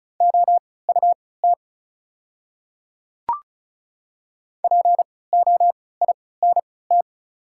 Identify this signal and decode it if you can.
Morse code, heard as a keyed tone, about 34 wpm, spaced as for 20 wpm, 700 Hz. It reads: OUT POINT
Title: 3x Character Spacing - Top 200 Words - 35wpm